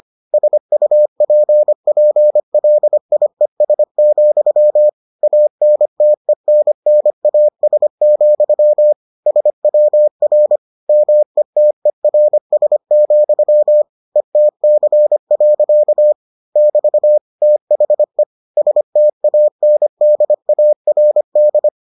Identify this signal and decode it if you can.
Morse code, heard as a keyed tone, about 25 wpm, 600 Hz. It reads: SUPPLIES, ANTENNAS, SWR METERS, ETC. <BT> THE STANDARD